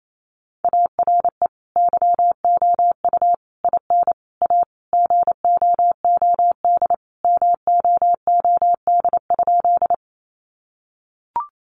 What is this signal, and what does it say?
Morse code, heard as a keyed tone, about 28 wpm, 700 Hz. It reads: ARE YOU IN A GOOD MOOD?
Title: Sentences from Top 1100 Words 28wpm